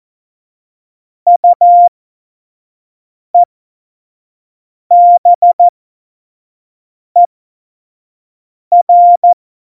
Morse code, heard as a keyed tone, about 14 wpm, spaced as for 5 wpm, 700 Hz. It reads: UEBER
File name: Morsekurs / Witze / Witz 08. Dezember2020, 25 F70